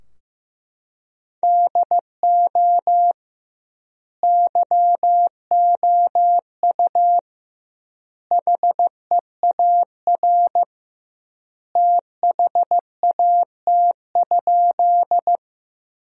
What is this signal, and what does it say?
Morse code, heard as a keyed tone, about 15 wpm, 700 Hz. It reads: DO YOU HEAR THAT?
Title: 2x Word-Spacing - Sentences from Top 300 Words - 15wpm